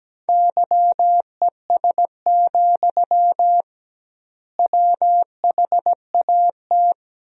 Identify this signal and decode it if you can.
Morse code, heard as a keyed tone, about 17 wpm, 700 Hz. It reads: YES, WHAT
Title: 2x Word-Spacing - Sentences from Top 500 Words - 17wpm